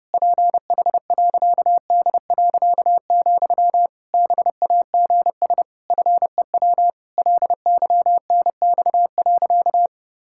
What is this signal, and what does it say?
Morse code, heard as a keyed tone, about 32 wpm, 700 Hz. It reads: PH.D., BAGS FEW LYNX.
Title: Pangram Challenge v2 - 30wpm